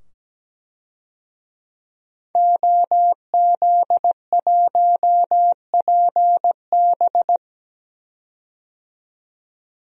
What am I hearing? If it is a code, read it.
Morse code, heard as a keyed tone, about 17 wpm, 700 Hz. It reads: OZ1PB
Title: DXCC-100 Call Signs - 5 Characters - 17wpm